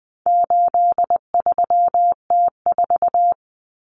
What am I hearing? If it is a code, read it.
Morse code, heard as a keyed tone, about 20 wpm, 700 Hz. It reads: 83T4
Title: Common Cut Numbers - 4-Digit 20wpm